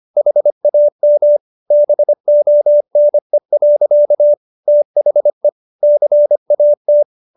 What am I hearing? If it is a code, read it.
Morse code, heard as a keyed tone, about 25 wpm, 590 Hz. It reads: HAM BONE. THE CAT